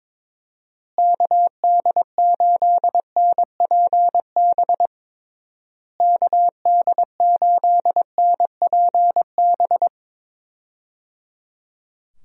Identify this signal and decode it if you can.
Morse code, heard as a keyed tone, about 22 wpm, 700 Hz. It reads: KD8NPB KD8NPB
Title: USA Call Signs - 6 Characters - Repeated 3x - 22wpm